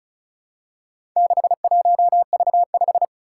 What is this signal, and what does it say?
Morse code, heard as a keyed tone, about 36 wpm, 700 Hz. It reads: 61V5
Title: Four Challenging Letter-Number Combinations 35wpm